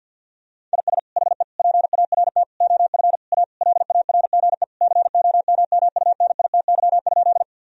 Morse code, heard as a keyed tone, about 50 wpm, 700 Hz. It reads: IS HE PART OF A LARGE COMMUNITY?
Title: Sentences from Top 1000 Words - Review Part 4 - 50wpm